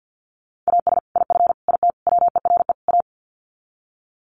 Morse code, heard as a keyed tone, about 50 wpm, 700 Hz. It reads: AS IF IT WERE A
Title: Sets of 5 Words 50wpm